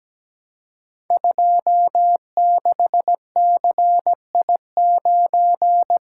Morse code, heard as a keyed tone, about 17 wpm, 690 Hz. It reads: 26CI9
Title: Code Groups - 5 Letter:Numbers - 17wpm